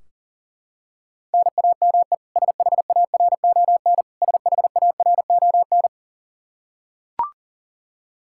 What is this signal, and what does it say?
Morse code, heard as a keyed tone, about 40 wpm, 700 Hz. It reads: NAME SHARON SHARON
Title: K1USN SST QSO Practice 40wpm